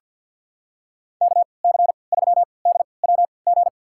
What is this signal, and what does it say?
Morse code, heard as a keyed tone, about 50 wpm, spaced as for 28 wpm, 700 Hz. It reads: KC3DWG
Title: USA Call Signs - 6 Characters - 3x Character Spacing - 50wpm